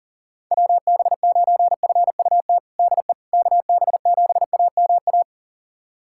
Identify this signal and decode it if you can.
Morse code, heard as a keyed tone, about 40 wpm, 700 Hz. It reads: WB9FUT DE KB7AMA
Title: USA Callsign DE Callsign - 6 Characters - 40wpm